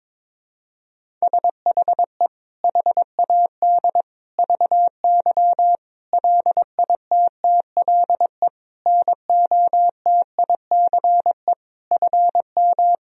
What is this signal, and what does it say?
Morse code, heard as a keyed tone, about 22 wpm, 700 Hz. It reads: SHE HAD VY LITTLE NOTICE FM
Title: Sentences from Top 400 Words and Abbreviations 22wpm